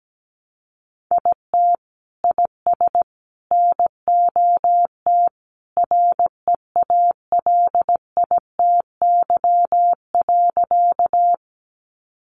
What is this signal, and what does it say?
Morse code, heard as a keyed tone, about 17 wpm, 700 Hz. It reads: IT IS NOT REALITY.
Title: Sentences from Top 1200 Words 17wpm